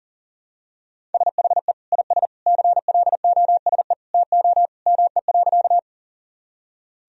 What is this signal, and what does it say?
Morse code, heard as a keyed tone, about 40 wpm, 700 Hz. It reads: SHE IS CLOSE TO ME.
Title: Sentences from Top 1000 Words - Review Part 1 - 40wpm